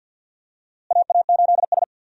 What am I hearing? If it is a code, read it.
Morse code, heard as a keyed tone, about 52 wpm, 700 Hz. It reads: AA8S
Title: USA Call Signs - 4 Characters - Repeated 3x - 50wpm